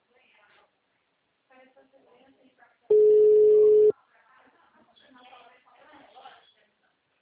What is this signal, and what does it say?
Ringback tone (Europe, 425 Hz), with speech.